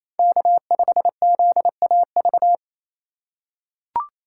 Morse code, heard as a keyed tone, about 28 wpm, 700 Hz. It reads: K5ZAV
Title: USA Call Signs - 4x Character Spacing - 28wpm